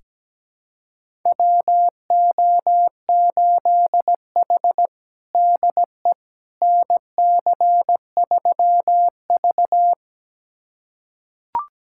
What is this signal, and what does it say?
Morse code, heard as a keyed tone, about 17 wpm, 700 Hz. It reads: WO8H DE NC3V
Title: USA Callsign DE Callsign - 4 Characters - 17wpm